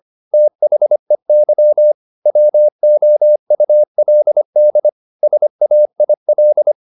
Morse code, heard as a keyed tone, about 25 wpm, 600 Hz. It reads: THEY WOULD SAIL